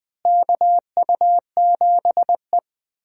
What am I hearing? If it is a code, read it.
Morse code, heard as a keyed tone, about 20 wpm, 700 Hz. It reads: KU7E
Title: USA Call Signs - 4 Characters - Repeated 2x - 20wpm